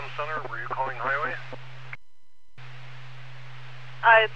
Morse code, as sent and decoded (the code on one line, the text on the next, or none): none